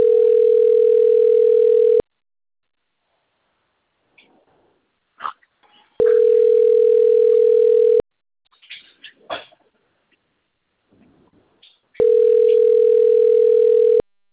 Ringback tone (North America), 440 + 480 Hz.